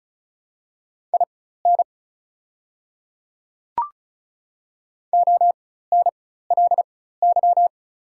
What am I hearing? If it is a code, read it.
Morse code, heard as a keyed tone, about 34 wpm, spaced as for 16 wpm, 690 Hz. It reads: IN ONLY
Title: 4x Character Spacing - Top 200 Words - 35wpm